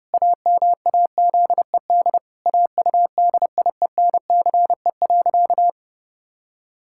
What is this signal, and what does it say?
Morse code, heard as a keyed tone, about 30 wpm, 700 Hz. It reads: AMAZED AUDIENCE.
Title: Pangram Challenge v2 - 30wpm